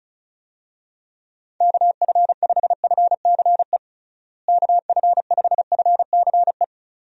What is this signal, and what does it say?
Morse code, heard as a keyed tone, about 36 wpm, 700 Hz. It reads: KF5FCE KF5FCE
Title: USA Call Signs - 6 Characters - Repeated 2x - 35wpm